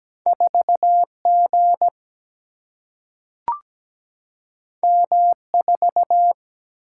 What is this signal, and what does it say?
Morse code, heard as a keyed tone, about 17 wpm, 700 Hz. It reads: 4 G M 4
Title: Single-Letter Single-Number 17wpm